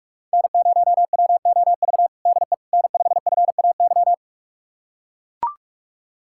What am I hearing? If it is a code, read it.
Morse code, heard as a keyed tone, about 42 wpm, 700 Hz. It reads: N0WOV DE N5FAY
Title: USA Callsign DE Callsign - 5 Characters - 45wpm